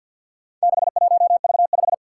Morse code, heard as a keyed tone, about 50 wpm, 700 Hz. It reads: 61V5
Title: Four Challenging Letter-Number Combinations 50wpm